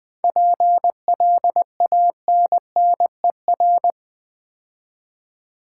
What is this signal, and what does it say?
Morse code, heard as a keyed tone, about 20 wpm, 700 Hz. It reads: PLANNER
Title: Difficult Words with Hints - 20wpm